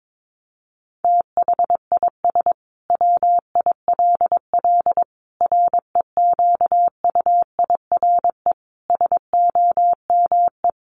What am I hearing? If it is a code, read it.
Morse code, heard as a keyed tone, about 22 wpm, 700 Hz. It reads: THIS WILL REQUIRE SOME
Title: Sentences from Top 900 Words 22wpm